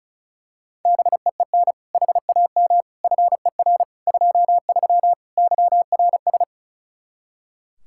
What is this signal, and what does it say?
Morse code, heard as a keyed tone, about 36 wpm, 700 Hz. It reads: BEEN HAM FER 23 YRS
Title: Practice QSO - Exchange 3 - More Info - 35wpm